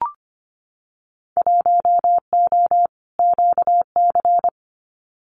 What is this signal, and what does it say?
Morse code, heard as a keyed tone, about 25 wpm, 700 Hz. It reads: 1O QC
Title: Winter Field Day Exchanges 25wpm